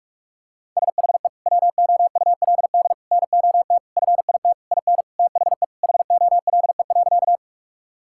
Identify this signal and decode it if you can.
Morse code, heard as a keyed tone, about 41 wpm, 700 Hz. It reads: SHE WOULD NOT FIT IN THE HOLE.
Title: Sentences from Top 1600 Words 45wpm